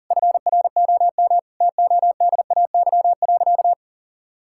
Morse code, heard as a keyed tone, about 37 wpm, 700 Hz. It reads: FROM TODAY.
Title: Sentences from Top 1000 Words - Review Part 5 - 40wpm